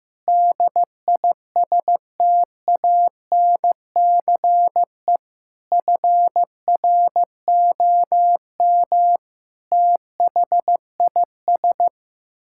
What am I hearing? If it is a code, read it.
Morse code, heard as a keyed tone, about 16 wpm, 700 Hz. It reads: DISTANCE FROM THIS
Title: Sentences from Top 1600 Words 15wpm